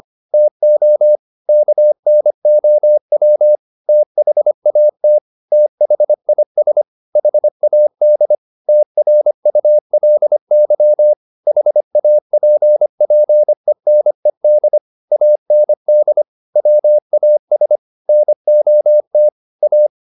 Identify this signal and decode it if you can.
Morse code, heard as a keyed tone, about 25 wpm, 600 Hz. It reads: TO KNOW THAT THIS HAD TRULY HAPPENED AND WAS NOT A